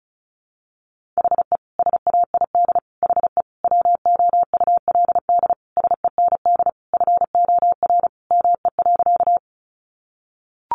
Morse code, heard as a keyed tone, about 34 wpm, 700 Hz. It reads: HE SAID HE WOULD SEND FOR ME.
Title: Sentences from Top 1500 Words 35wpm